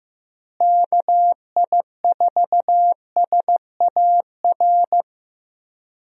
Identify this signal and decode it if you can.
Morse code, heard as a keyed tone, about 15 wpm, 700 Hz. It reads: KI4SAR
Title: USA Call Signs - 6 Characters - Repeated 3x - 15wpm